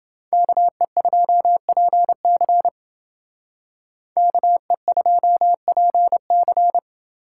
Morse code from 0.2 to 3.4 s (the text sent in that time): KE2PC